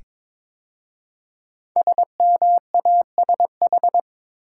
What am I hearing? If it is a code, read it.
Morse code, heard as a keyed tone, about 22 wpm, 700 Hz. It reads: SMASH
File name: 5 Letter Words - First 3 Letters Given - 22wpm